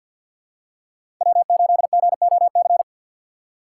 Morse code, heard as a keyed tone, about 53 wpm, 700 Hz. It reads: W8GOC
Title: USA Call Signs - 5 Characters - 2x Character Spacing - 50wpm